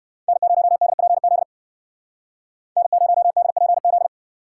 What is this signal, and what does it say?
Morse code, heard as a keyed tone, about 56 wpm, 700 Hz. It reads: N0DPZ N0DPZ